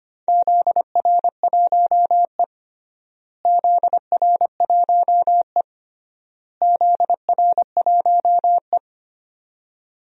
Morse code, heard as a keyed tone, about 25 wpm, 700 Hz. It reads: ZR1E ZR1E ZR1E
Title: DXCC-100 Call Signs - 4 Characters - Repeated 3x - 25wpm